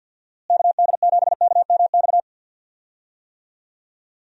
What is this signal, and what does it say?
Morse code, heard as a keyed tone, about 51 wpm, 700 Hz. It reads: KD7KMX
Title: USA Call Signs - 6 Characters - Repeated 2x - 50wpm